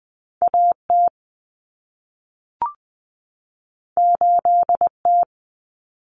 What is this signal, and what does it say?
Morse code, heard as a keyed tone, about 20 wpm, 700 Hz. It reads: AT 8T
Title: Common Cut Numbers - 2-Digit 20wpm